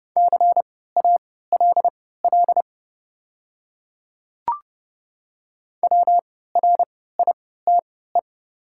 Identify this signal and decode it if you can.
Morse code, heard as a keyed tone, about 30 wpm, spaced as for 17 wpm, 700 Hz. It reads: CALL WRITE
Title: 3x Character Spacing - Top 100 Words - 30wpm